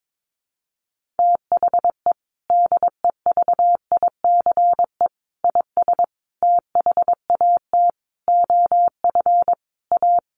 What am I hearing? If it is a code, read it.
Morse code, heard as a keyed tone, about 22 wpm, 700 Hz. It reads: THE DEVICE IS THAT OF A
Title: Sentences from Top 1500 Words 22wpm